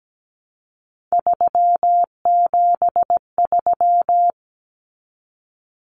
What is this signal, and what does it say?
Morse code, heard as a keyed tone, about 17 wpm, 700 Hz. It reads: 373
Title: Numbers - 3-Digit 17wpm